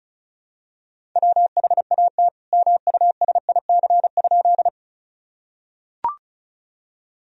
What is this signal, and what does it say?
Morse code, heard as a keyed tone, about 34 wpm, 700 Hz. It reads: WHAT MUSIC?
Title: Sentences from Top 1000 Words - Review Part 1 - 35wpm